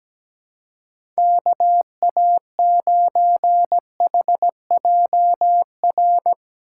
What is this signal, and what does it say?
Morse code, heard as a keyed tone, about 17 wpm, 700 Hz. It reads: KA9HJR
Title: USA Call Signs - 6 Characters - 3x Character Spacing - 17wpm